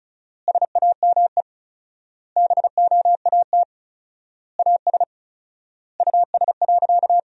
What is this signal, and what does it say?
Morse code, heard as a keyed tone, about 36 wpm, 700 Hz. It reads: SAME BOAT AS US.
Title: 4x Word-Spacing - Sentences from Top 500 Words - 35wpm